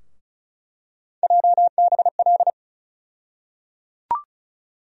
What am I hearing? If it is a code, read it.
Morse code, heard as a keyed tone, about 34 wpm, 700 Hz. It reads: JBL
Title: USA Call Sign Postfixes - 3 Letters 35wpm